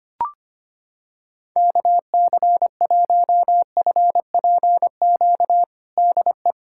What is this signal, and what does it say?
Morse code, heard as a keyed tone, about 25 wpm, 700 Hz. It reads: KC1FPQ DE
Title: USA Callsign DE Callsign 25wpm